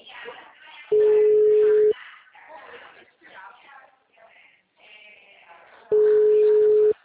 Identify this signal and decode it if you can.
Ringback tone (Europe, 425 Hz), with speech.
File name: out- -2008- - - .4991